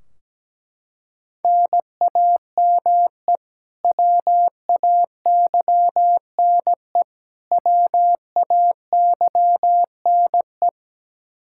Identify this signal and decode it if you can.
Morse code, heard as a keyed tone, about 17 wpm, 700 Hz. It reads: NAME WAYNE WAYNE
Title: Practice QSO - Exchange 1 - Beginning - 17wpm